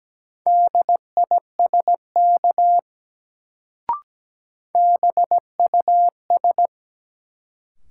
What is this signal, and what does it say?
Morse code, heard as a keyed tone, about 17 wpm, 700 Hz. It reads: DISK BUS
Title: Sentences from Top 1000 Words - Review Part 6 - New Word Drill - 17wpm